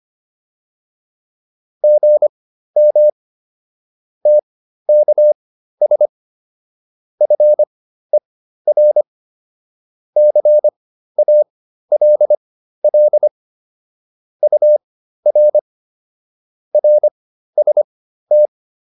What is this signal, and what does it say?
Morse code, heard as a keyed tone, about 25 wpm, spaced as for 13 wpm, 600 Hz. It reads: GM TKS FER CALL UR RST
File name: QSO405 13